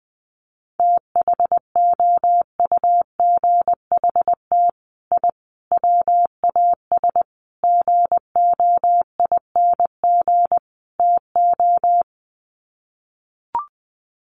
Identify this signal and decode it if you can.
Morse code, heard as a keyed tone, about 20 wpm, 700 Hz. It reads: THOUGHT I WAS GOING TO
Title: Sets of 5 Words Encore2 20wpm